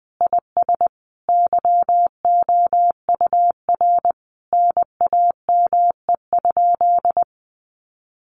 Morse code, heard as a keyed tone, about 20 wpm, 700 Hz. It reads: IS YOUR NAME?
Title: Sentences from Top 1200 Words 20wpm